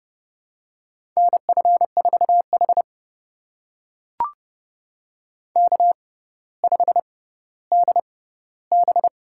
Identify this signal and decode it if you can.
Morse code, heard as a keyed tone, about 31 wpm, spaced as for 16 wpm, 700 Hz. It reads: NF4H K5DB
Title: USA Call Signs - 4 Characters - 6x Character Spacing - 30wpm